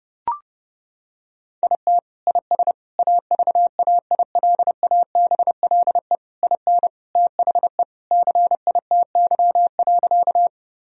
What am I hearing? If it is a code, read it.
Morse code, heard as a keyed tone, about 30 wpm, 700 Hz. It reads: IT IS AVAILABLE IN THE CITY.
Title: Sentences from Top 1000 Words - Review Part 3 - 30wpm